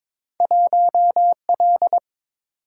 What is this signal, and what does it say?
Morse code, heard as a keyed tone, about 22 wpm, 700 Hz. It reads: 1L
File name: practice / 22wpm / Two Letter-Number Combinations - 22wpm